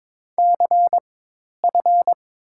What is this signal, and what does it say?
Morse code, heard as a keyed tone, about 23 wpm, spaced as for 10 wpm, 700 Hz. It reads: CF
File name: Top 90 QSO Elements - 4x Character Spacing - 22wpm